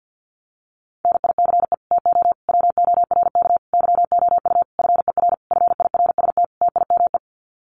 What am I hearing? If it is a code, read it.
Morse code, heard as a keyed tone, about 50 wpm, 700 Hz. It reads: NICE TO WORK YOU FER FIRST TIME
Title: Common QSO Phrases 50wpm